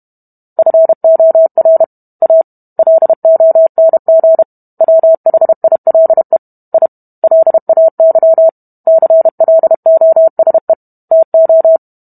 Morse code, heard as a keyed tone, about 32 wpm, 650 Hz. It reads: FOR A LONG WHILE I LAY CLOSE TO